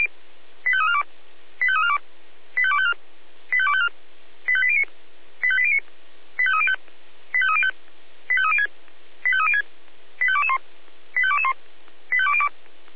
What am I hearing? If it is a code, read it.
Five-tone selective call (ZVEI), heard as quick runs of five tones, tones of about 70 ms. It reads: 85342 85342 85254 85254 85809 85809 85305 85305 85306 85306 85221 85221 85202